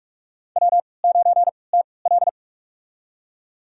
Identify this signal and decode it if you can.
Morse code, heard as a keyed tone, about 46 wpm, spaced as for 26 wpm, 700 Hz. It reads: W9TL
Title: USA Call Signs - 3x Character Spacing - 45wpm